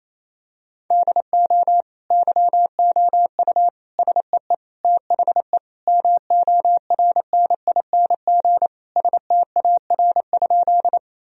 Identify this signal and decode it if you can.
Morse code, heard as a keyed tone, about 28 wpm, 700 Hz. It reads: DO YOU SEE THE MORNING STAR?
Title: Sentences from Top 1000 Words - Review Part 2 - 28wpm